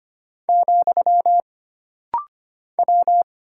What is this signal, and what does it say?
Morse code, heard as a keyed tone, about 25 wpm, 700 Hz. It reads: , W
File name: Single Character (Letter:Number:Punctuation:Prosign) - Rapid-Fire - 25wpm